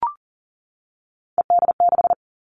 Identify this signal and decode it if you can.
Morse code, heard as a keyed tone, about 38 wpm, 690 Hz. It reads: ED6
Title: Callsign Prefixes for Top 100 DXCC - 40wpm